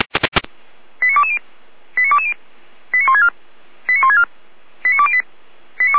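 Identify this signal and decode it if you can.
Five-tone selective call (ZVEI), heard as quick runs of five tones, tones of about 80 ms. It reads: 89220 89220 89164 89164 89208
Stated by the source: Five-tone ZVEI selective call received by SDR, from the Feuerwehr Orlamünde (fire brigade) audio server.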